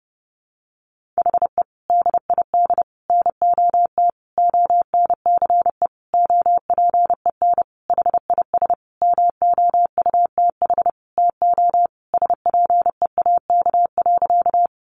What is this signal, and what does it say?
Morse code, heard as a keyed tone, about 30 wpm, 700 Hz. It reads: HE DID NOT ONCE OPEN HIS MOUTH TO SPEAK.
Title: Sentences from Top 1500 Words 30wpm